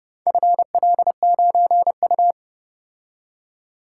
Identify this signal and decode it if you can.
Morse code, heard as a keyed tone, about 30 wpm, 700 Hz. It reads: FL9U